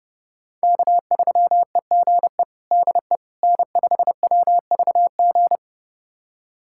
Morse code, heard as a keyed tone, about 30 wpm, 700 Hz. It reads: K3EGE DE N5WVG
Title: USA Callsign DE Callsign - 5 Characters - 30wpm